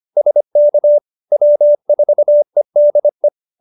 Morse code, heard as a keyed tone, about 25 wpm, 590 Hz. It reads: SK W4EDE